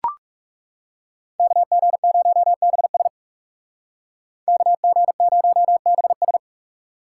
Morse code, heard as a keyed tone, about 39 wpm, 700 Hz. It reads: KG0BS KG0BS